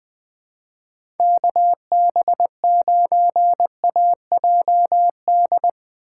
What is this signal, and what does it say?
Morse code, heard as a keyed tone, about 21 wpm, 700 Hz. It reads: KB9AJD
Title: USA Call Signs - 6 Characters - 3x Character Spacing - 20wpm